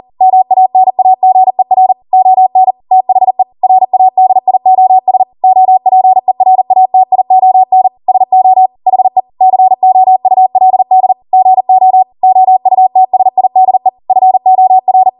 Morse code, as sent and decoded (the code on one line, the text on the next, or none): -- .- -. .- --. .-. / --- -. / - .... / .-. .- -.. .. --- ... / --- .--. .-. .- - .. --- -. / ... --- / .... / -.-. --- ..- .-.. -.. / --. --- / --- ..- - ... .. -.. / ..-. --- .-.
MANAGER ON THE RADIOS OPERATION SO HE COULD GO OUTSIDE FOR